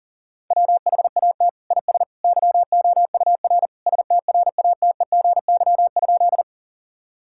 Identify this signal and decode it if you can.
Morse code, heard as a keyed tone, about 40 wpm, 690 Hz. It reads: WHAT IS YOUR STRATEGY?
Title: Sentences from Top 1000 Words - Review Part 5 - 40wpm